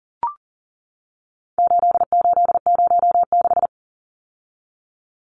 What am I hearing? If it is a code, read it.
Morse code, heard as a keyed tone, about 40 wpm, 700 Hz. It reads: 8806
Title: Numbers - 4-Digit 40wpm